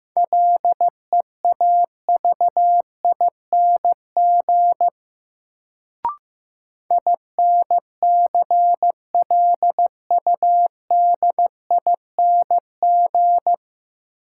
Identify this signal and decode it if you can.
Morse code, heard as a keyed tone, about 15 wpm, 700 Hz. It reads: LEAVING INCLUDING
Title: Sentences from Top 1000 Words - Review Part 4 - New Word Drill - 15wpm